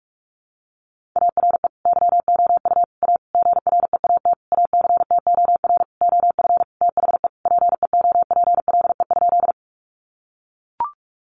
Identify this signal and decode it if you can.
Morse code, heard as a keyed tone, about 44 wpm, 700 Hz. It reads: ARE YOU A GREAT ACTOR OF THE PEOPLE?
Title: Sentences from Top 1500 Words 45wpm